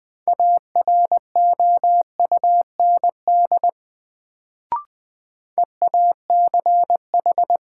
Morse code, heard as a keyed tone, about 20 wpm, 700 Hz. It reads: AROUND EACH
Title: Sets of 2 Words - New Word Drill - 20wpm